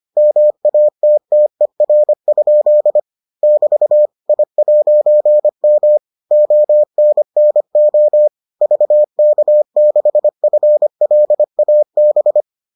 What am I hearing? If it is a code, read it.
Morse code, heard as a keyed tone, about 25 wpm, 600 Hz. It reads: MATTER? <BT> I'M ONNO VK6FLAB